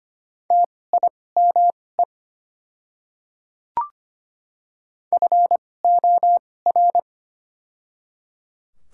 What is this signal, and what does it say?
Morse code, heard as a keyed tone, about 25 wpm, spaced as for 18 wpm, 690 Hz. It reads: TIME FOR
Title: 2x Character Spacing - Top 100 Words - 25wpm